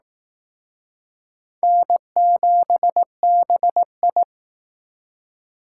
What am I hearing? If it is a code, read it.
Morse code, heard as a keyed tone, about 19 wpm, 700 Hz. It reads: N7BI